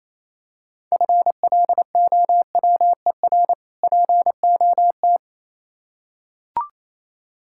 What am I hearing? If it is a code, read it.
Morse code, heard as a keyed tone, about 27 wpm, 690 Hz. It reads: FLOWER POT
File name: Easy Words - Speed-Racing with Hints - 22wpm